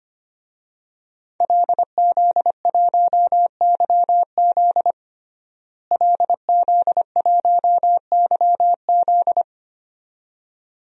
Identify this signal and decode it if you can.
Morse code, heard as a keyed tone, about 25 wpm, 700 Hz. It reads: LZ1YZ LZ1YZ
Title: DXCC-100 Call Signs - 5 Characters - Repeated 2x - 25wpm